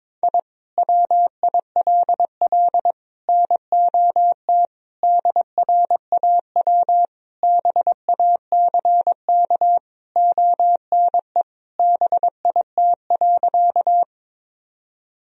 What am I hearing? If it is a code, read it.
Morse code, heard as a keyed tone, about 22 wpm, 700 Hz. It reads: I WILL NOT DRAW BACK ONE BIT.
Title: Sentences from Top 1000 Words - Review Part 1 - 22wpm